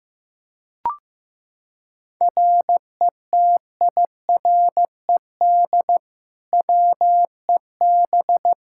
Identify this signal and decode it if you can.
Morse code, heard as a keyed tone, about 15 wpm, 700 Hz. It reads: RETIRED WEB
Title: K1USN SST QSO Practice 15wpm